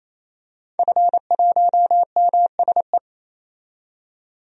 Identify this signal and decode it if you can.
Morse code, heard as a keyed tone, about 28 wpm, 700 Hz. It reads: F1MSE
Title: DXCC-100 Call Signs - 5 Characters - 28wpm